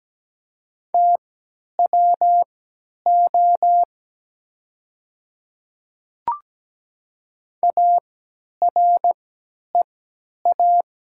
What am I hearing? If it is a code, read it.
Morse code, heard as a keyed tone, about 17 wpm, spaced as for 10 wpm, 700 Hz. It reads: TWO AREA
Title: 3x Character Spacing - Top 500 Words - 17wpm